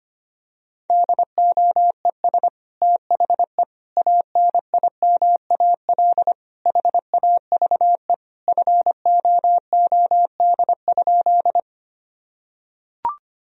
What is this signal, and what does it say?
Morse code, heard as a keyed tone, about 25 wpm, 700 Hz. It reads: DOES THE ANIMAL HAVE FOOD?
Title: Sentences from Top 1000 Words - Review Part 1 - 25wpm